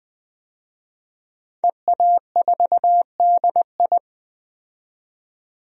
Morse code, heard as a keyed tone, about 20 wpm, 700 Hz. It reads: EA4DI
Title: DXCC-100 Call Signs - 5 Characters - 20wpm